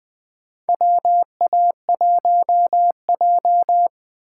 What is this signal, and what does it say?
Morse code, heard as a keyed tone, about 21 wpm, 700 Hz. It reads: WA1J
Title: USA Call Signs - 4 Characters - 4x Character Spacing - 20wpm